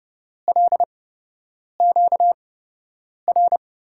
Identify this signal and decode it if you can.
Morse code, heard as a keyed tone, about 30 wpm, spaced as for 8 wpm, 700 Hz. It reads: LQR
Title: USA Call Signs - 8x Character Spacing - 30wpm